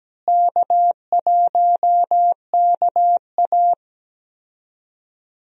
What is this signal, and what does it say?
Morse code, heard as a keyed tone, about 17 wpm, 700 Hz. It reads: K1KA
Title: USA Call Signs - 4 Characters - Repeated 2x - 17wpm